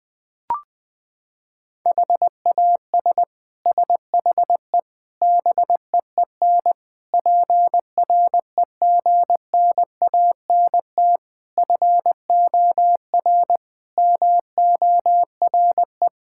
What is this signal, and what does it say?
Morse code, heard as a keyed tone, about 21 wpm, 700 Hz. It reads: HAS SHE BEEN PREGNANT FOR MORE
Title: Sentences from Top 1000 Words - Review Part 6 - 20wpm